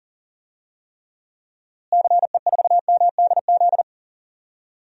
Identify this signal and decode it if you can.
Morse code, heard as a keyed tone, about 40 wpm, 700 Hz. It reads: CE4MDZ
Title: DXCC-100 Call Signs - 6 Characters - Repeated 3x - 40wpm